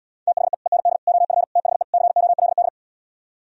Morse code, heard as a keyed tone, about 77 wpm, 700 Hz. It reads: THE END OF THE COLD